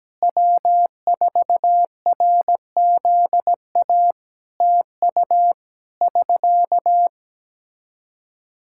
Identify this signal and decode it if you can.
Morse code, heard as a keyed tone, about 17 wpm, 700 Hz. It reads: W4RZA TU <SK>
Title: Practice QSO - Exchange 4 - Ending - 17wpm